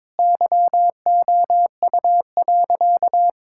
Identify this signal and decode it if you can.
Morse code, heard as a keyed tone, about 21 wpm, 700 Hz. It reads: YOU.
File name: practice / 22wpm / Sentences with 7-Word Phrases - 22wpm